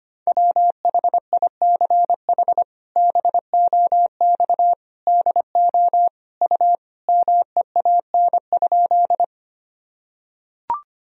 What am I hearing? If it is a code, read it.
Morse code, heard as a keyed tone, about 25 wpm, 700 Hz. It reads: WHICH BOX DO U MEAN?
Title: Sentences from Top 500 Words and Abbreviations 25wpm